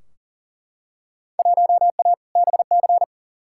Morse code, heard as a keyed tone, about 40 wpm, 700 Hz. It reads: 1A BC
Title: Field Day Exchanges - 40wpm